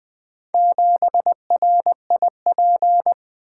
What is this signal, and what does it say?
Morse code, heard as a keyed tone, about 20 wpm, 700 Hz. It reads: 7RIP
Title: Four Letter-Number Combinations - 20wpm